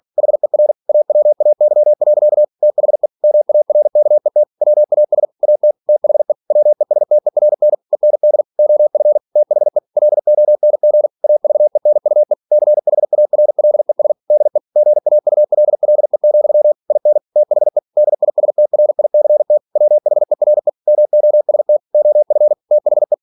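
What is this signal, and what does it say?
Morse code, heard as a keyed tone, about 46 wpm, 600 Hz. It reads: HER AWAY. THE MARKET WAS AT THE WESTERN END OF THE LONG AVENUE CHARLES DE GAULLE, IN THE DISTRICT WHERE MOST OF THE
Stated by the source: Practice text from an English novel sent at 47 WPM.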